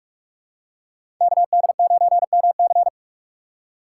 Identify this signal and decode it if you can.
Morse code, heard as a keyed tone, about 45 wpm, 700 Hz. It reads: KD9MC
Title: USA Call Signs - 5 Characters - 6x Character Spacing - 45wpm